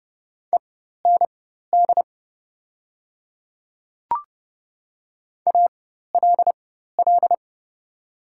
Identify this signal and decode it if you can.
Morse code, heard as a keyed tone, about 30 wpm, spaced as for 14 wpm, 700 Hz. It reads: END ALL
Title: 4x Character Spacing - Top 300 Words - 30wpm